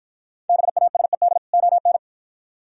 Morse code, heard as a keyed tone, about 52 wpm, 700 Hz. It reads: BASED ON